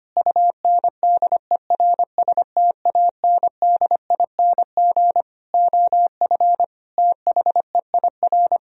Morse code, heard as a keyed tone, about 25 wpm, 700 Hz. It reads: UNDERSTANDING OF THEIR
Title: Sentences from Top 1000 Words - Review Part 3 - 25wpm